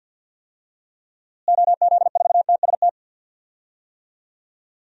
Morse code, heard as a keyed tone, about 47 wpm, 700 Hz. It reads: OZ4TST
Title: DXCC-100 Call Signs - 6 Characters - Repeated 3x - 50wpm